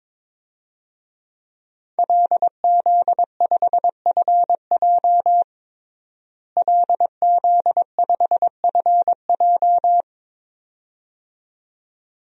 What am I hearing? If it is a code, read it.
Morse code, heard as a keyed tone, about 22 wpm, 700 Hz. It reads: LZ5FJ LZ5FJ
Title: DXCC-100 Call Signs - 5 Characters - Repeated 2x - 22wpm